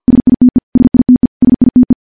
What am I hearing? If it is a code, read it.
Morse code, heard as a keyed tone, about 50 wpm, 250 Hz. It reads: SITE SITE SITE
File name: morse / morse-training-50WPM